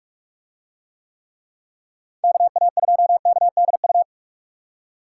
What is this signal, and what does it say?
Morse code, heard as a keyed tone, about 46 wpm, 700 Hz. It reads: KA2KDU